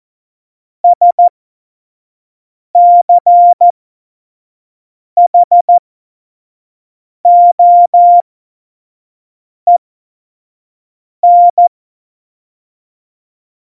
Morse code, heard as a keyed tone, about 14 wpm, spaced as for 5 wpm, 700 Hz. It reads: SCHOEN
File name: Morsekurs / Witze / Witz 27. Februar 2018, 25 F70